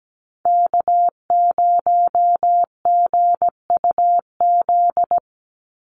Morse code, heard as a keyed tone, about 17 wpm, 700 Hz. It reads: K0GUZ
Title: Call Signs Repeated 2x - 17wpm